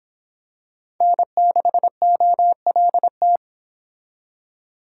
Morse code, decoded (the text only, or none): N6OLT